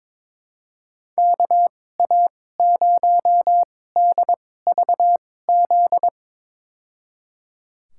Morse code, heard as a keyed tone, about 21 wpm, spaced as for 16 wpm, 700 Hz. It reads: KA0DVZ
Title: USA Call Signs - 6 Characters - 2x Character Spacing - 22wpm